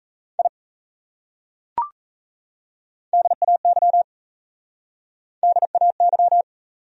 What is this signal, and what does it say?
Morse code, heard as a keyed tone, about 34 wpm, 700 Hz. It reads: I DAY DAY
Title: Speed-Racing Top 200 Words 28wpm